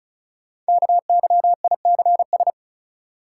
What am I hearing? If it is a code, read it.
Morse code, heard as a keyed tone, about 36 wpm, 700 Hz. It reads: KYICS
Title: Code Groups - 5 Letters - 35wpm